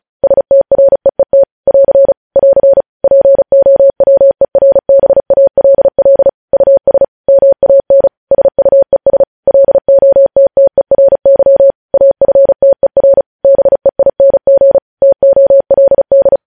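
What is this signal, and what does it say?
Morse code, heard as a keyed tone, about 36 wpm, 550 Hz. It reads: STREET <AR> <AR> POWERBALL US MAN SUES LOTTERY AFTER BEING TOLD
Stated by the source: Morse code news headlines sent at 35.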